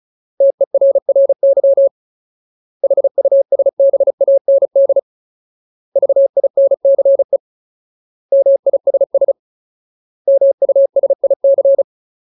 Morse code, heard as a keyed tone, about 34 wpm, 550 Hz. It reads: TERRY HUSBAND VINCE MISS MUSIC
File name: Ultimate Word List / 35WPM Ultimate